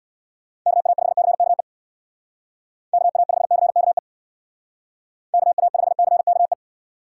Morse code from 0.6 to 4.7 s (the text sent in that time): KA5CKE KA5CKE